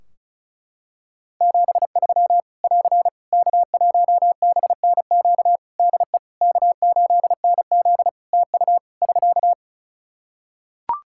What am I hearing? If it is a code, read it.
Morse code, heard as a keyed tone, about 36 wpm, 700 Hz. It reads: 73 <AR> K1BNQ DE K8NZ TU <SK>
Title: Practice QSO - Exchange 4 - Ending - 35wpm